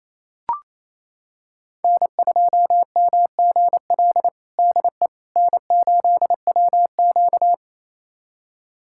Morse code, heard as a keyed tone, about 28 wpm, 700 Hz. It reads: N2MGL DE N8WQ
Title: USA Callsign DE Callsign 28wpm